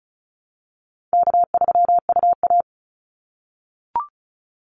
Morse code, heard as a keyed tone, about 34 wpm, 700 Hz. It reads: K3UA
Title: Call Signs - 35wpm